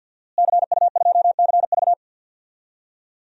Morse code, heard as a keyed tone, about 47 wpm, 700 Hz. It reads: CU2CV
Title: DXCC-100 Call Signs - 5 Characters - Repeated 3x - 50wpm